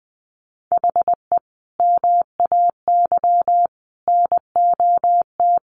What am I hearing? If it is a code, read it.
Morse code, heard as a keyed tone, about 20 wpm, 700 Hz. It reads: HE MAY NOT
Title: Sentences from Top 1500 Words 20wpm